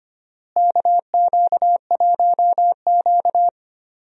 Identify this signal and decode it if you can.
Morse code, heard as a keyed tone, about 25 wpm, 700 Hz. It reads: KQ1Q
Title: USA Call Signs - 4 Characters - Repeated 3x - 25wpm